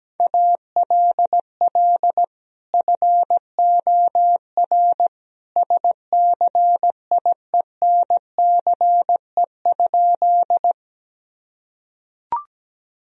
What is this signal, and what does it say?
Morse code, heard as a keyed tone, about 17 wpm, 700 Hz. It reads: ALL FOR SCIENCE?
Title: Sentences from Top 1000 Words - Review Part 2 - 17wpm